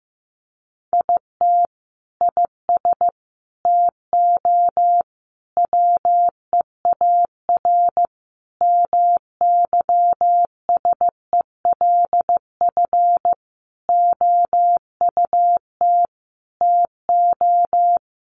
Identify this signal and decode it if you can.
Morse code, heard as a keyed tone, about 15 wpm, 700 Hz. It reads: IT IS TO WEAR MYSELF OUT TO
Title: Sentences from Top 1500 Words 15wpm